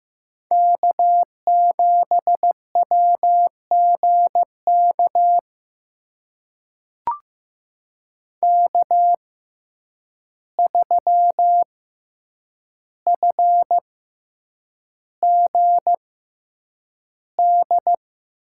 Morse code, heard as a keyed tone, about 15 wpm, spaced as for 8 wpm, 700 Hz. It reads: K7WGK K3FGD
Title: USA Call Signs - 5 Characters - 6x Character Spacing - 15wpm